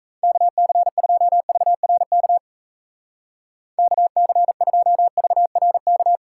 Morse code, heard as a keyed tone, about 36 wpm, 700 Hz. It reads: KC2VRK KC2VRK